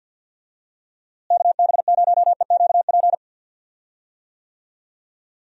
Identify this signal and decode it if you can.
Morse code, heard as a keyed tone, about 52 wpm, 700 Hz. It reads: KB0EQP
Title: USA Call Signs - 6 Characters - Repeated 2x - 50wpm